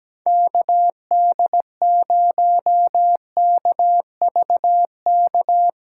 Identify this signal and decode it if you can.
Morse code, heard as a keyed tone, about 17 wpm, 700 Hz. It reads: KD0KVK
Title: USA Callsign DE Callsign - 6 Characters - 17wpm